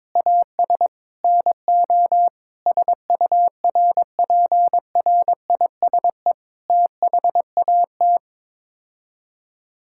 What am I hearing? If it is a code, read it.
Morse code, heard as a keyed tone, about 22 wpm, 700 Hz. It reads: AS NO SURPRISE THAT
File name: practice / 22wpm / Sets of 8 Words - 22wpm